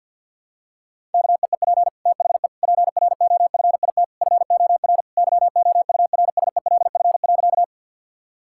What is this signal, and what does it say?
Morse code, heard as a keyed tone, about 51 wpm, 700 Hz. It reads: KEEP THE PROFIT FOR YOURSELF.
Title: Sentences from Top 1000 Words - Review Part 5 - 50wpm